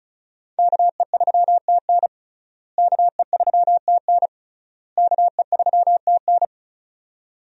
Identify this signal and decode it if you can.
Morse code, heard as a keyed tone, about 36 wpm, 700 Hz. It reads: KE3TN KE3TN KE3TN
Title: USA Call Signs - 5 Characters - Repeated 3x - 35wpm